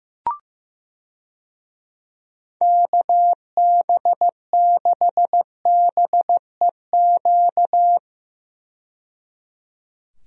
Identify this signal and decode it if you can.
Morse code, heard as a keyed tone, about 15 wpm, 700 Hz. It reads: KB6BEQ
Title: USA Call Signs - 6 Characters - 15wpm